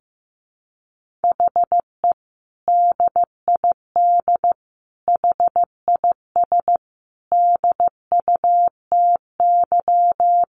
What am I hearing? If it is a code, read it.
Morse code, heard as a keyed tone, about 15 wpm, 700 Hz. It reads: HE DID HIS DUTY
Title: Sentences from Top 1500 Words 15wpm